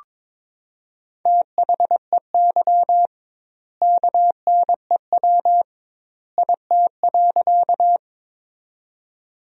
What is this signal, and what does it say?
Morse code, heard as a keyed tone, about 22 wpm, 700 Hz. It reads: THEY KNEW IT.
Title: 2x Word-Spacing - Sentences from Top 400 Words - 22wpm